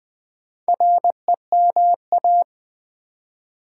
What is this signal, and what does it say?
Morse code, heard as a keyed tone, about 20 wpm, 700 Hz. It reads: REMA
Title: Build A Word - 2 Letters At A Time - 6-Letter Words - 20wpm